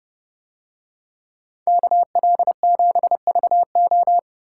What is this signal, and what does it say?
Morse code, heard as a keyed tone, about 30 wpm, 700 Hz. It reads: KL7VO